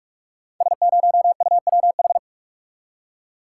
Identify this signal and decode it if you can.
Morse code, heard as a keyed tone, about 44 wpm, 690 Hz. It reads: S0UWH
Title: Code Groups - 5 Letter:Numbers - 45wpm